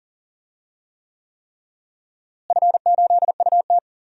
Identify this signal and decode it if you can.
Morse code, heard as a keyed tone, about 40 wpm, 700 Hz. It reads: F8UT
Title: DXCC-100 Call Signs - 4 Characters - Repeated 3x - 40wpm